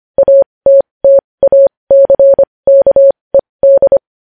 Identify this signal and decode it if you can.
Morse code, heard as a keyed tone, about 25 wpm, spaced as for 20 wpm, 550 Hz. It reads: ATTACKED
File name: podcast / News headlines-1x-25x20